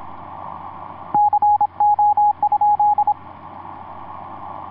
Morse code, heard as a keyed tone, about 26 wpm, 830 Hz, with radio noise. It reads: CO?
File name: zs2dl ea2ok etc 12m cw 07mar2010 1637